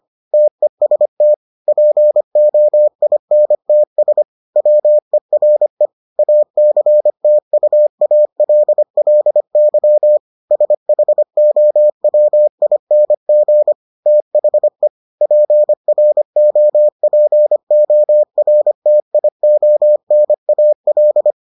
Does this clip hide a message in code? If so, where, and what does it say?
Yes: Morse code from 0.3 to 21.3 s, heard as a keyed tone, about 25 wpm, 600 Hz. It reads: TEST POINTS WERE ACTUALLY SHOWING THE PROPORTIONAL